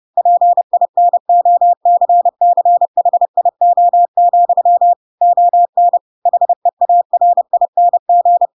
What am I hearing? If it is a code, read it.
Morse code, heard as a keyed tone, about 30 wpm, 700 Hz. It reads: PINOCCHIO, ON HEARING
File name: Pinocchio / Pinocchio ch3 30 30 ik1hge